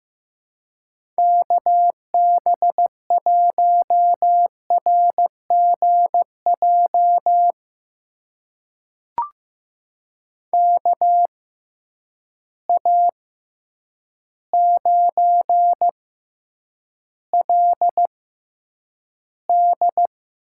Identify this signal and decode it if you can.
Morse code, heard as a keyed tone, about 15 wpm, spaced as for 8 wpm, 700 Hz. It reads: KB1RGJ KA9LD
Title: USA Call Signs - 6 Characters - 6x Character Spacing - 15wpm